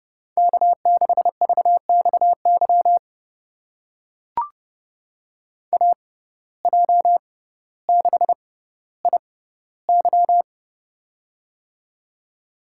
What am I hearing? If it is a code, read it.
Morse code, heard as a keyed tone, about 31 wpm, spaced as for 16 wpm, 700 Hz. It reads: K6VXY AJ6IY
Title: USA Call Signs - 6x Character Spacing - 30wpm